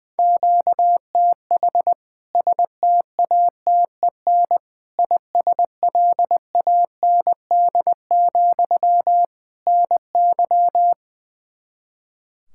Morse code, heard as a keyed tone, about 21 wpm, 700 Hz. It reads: QTH STATEN ISLAND, NY
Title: K1USN SST QSO Practice 20wpm